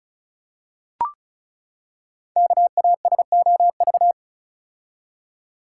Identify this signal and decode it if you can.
Morse code, heard as a keyed tone, about 36 wpm, 700 Hz. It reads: KASOV
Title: Code Groups - 5 Letters - 35wpm